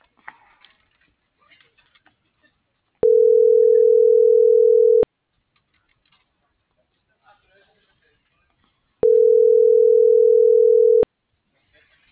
Ringback tone (North America), 440 + 480 Hz.